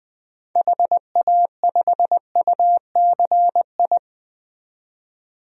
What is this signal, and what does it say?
Morse code, heard as a keyed tone, about 20 wpm, 700 Hz. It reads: HA5UCI
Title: DXCC-100 Call Signs - 6 Characters - 20wpm